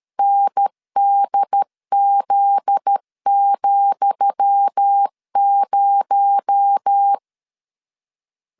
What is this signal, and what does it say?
Morse code, heard as a keyed tone, about 13 wpm, 800 Hz. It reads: NDZ,0
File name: LEZIONI / 10 SIGNS / 10D COMA 12wpm 2544